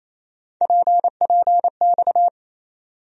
Morse code, heard as a keyed tone, about 28 wpm, 700 Hz. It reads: PPX
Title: USA Call Sign Postfixes - 3 Letters 28wpm